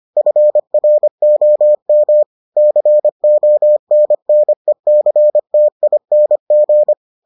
Morse code, heard as a keyed tone, about 25 wpm, 600 Hz. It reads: FROM CONNECTING